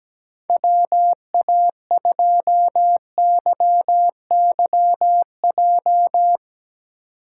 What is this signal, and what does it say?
Morse code, heard as a keyed tone, about 17 wpm, 700 Hz. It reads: WA2YYJ
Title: USA Call Signs - 6 Characters - 2x Character Spacing - 17wpm